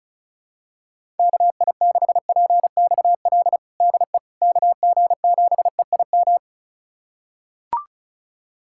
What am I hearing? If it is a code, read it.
Morse code, heard as a keyed tone, about 36 wpm, 700 Hz. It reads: KI6PXL DE KG7EIM
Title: USA Callsign DE Callsign 35wpm